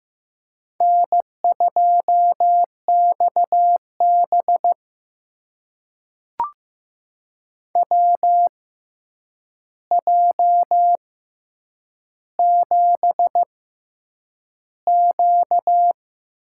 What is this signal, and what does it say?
Morse code, heard as a keyed tone, about 15 wpm, spaced as for 8 wpm, 700 Hz. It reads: N2XB WJ7Q
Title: USA Call Signs - 4 Characters - 6x Character Spacing - 15wpm